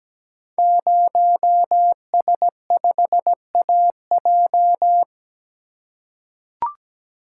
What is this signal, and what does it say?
Morse code, heard as a keyed tone, about 17 wpm, 700 Hz. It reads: 0S5AJ
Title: Code Groups - 5 Letter:Numbers - 17wpm